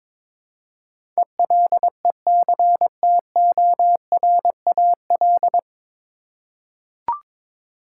Morse code, heard as a keyed tone, about 22 wpm, 700 Hz. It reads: ELECTORAL
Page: Morse Code Ninja practice 22wpm